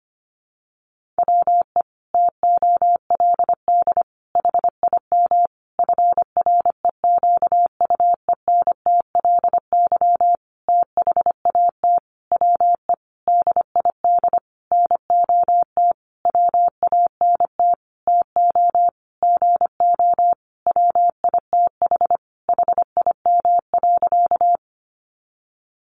Morse code, heard as a keyed tone, about 25 wpm, 700 Hz. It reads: WE TOLD HIM FREQUENTLY THAT WE DID NOT WANT TO GO WITH HIM.
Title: Sentences from Top 1300 Words 25wpm